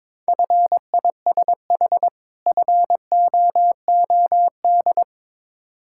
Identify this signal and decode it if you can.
Morse code, heard as a keyed tone, about 21 wpm, 690 Hz. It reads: FISH FOOD
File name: Difficult Words with Hints - 22wpm